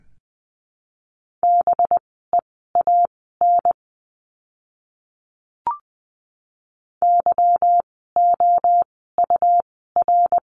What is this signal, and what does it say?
Morse code, heard as a keyed tone, about 20 wpm, spaced as for 14 wpm, 700 Hz. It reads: BEAN YOUR
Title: 185 Lesson 30f - Letter b - Words 20wpm